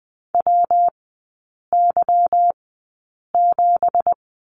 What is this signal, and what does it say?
Morse code, heard as a keyed tone, about 20 wpm, spaced as for 8 wpm, 690 Hz. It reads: WY7
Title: 166 Lesson 27e - Number 7 - Three Characters 20wpm